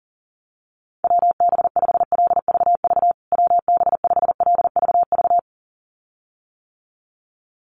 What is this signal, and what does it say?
Morse code, heard as a keyed tone, about 40 wpm, 700 Hz. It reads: WB5LVV WB5LVV
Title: USA Call Signs Repeated 2x 40wpm